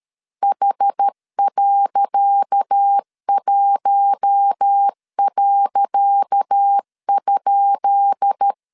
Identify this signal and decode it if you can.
Morse code, heard as a keyed tone, about 13 wpm, 800 Hz. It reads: H.1.?
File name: LEZIONI / 10 SIGNS / 10C DOT 12wpm 2451